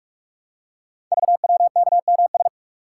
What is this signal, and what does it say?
Morse code, heard as a keyed tone, about 44 wpm, 700 Hz. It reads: VWKMS
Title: Code Groups - 5 Letters - 45wpm